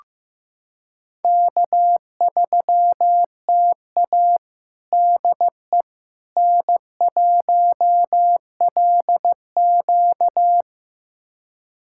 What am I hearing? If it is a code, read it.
Morse code, heard as a keyed tone, about 15 wpm, 700 Hz. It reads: K3TA DE N1LQ
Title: USA Callsign DE Callsign - 4 Characters - 15wpm